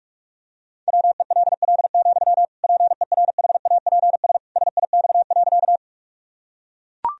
Morse code, heard as a keyed tone, about 41 wpm, 700 Hz. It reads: WELL, PERHAPS SIX.
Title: Sentences from Top 1000 Words - Review Part 2 - 45wpm